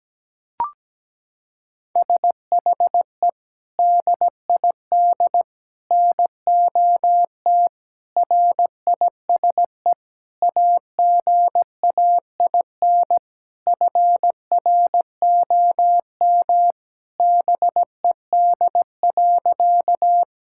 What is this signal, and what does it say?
Morse code, heard as a keyed tone, about 17 wpm, 700 Hz. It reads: SHE DID NOT RISE AGAIN FROM BED.
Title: Sentences from Top 1000 Words - Review Part 6 - 17wpm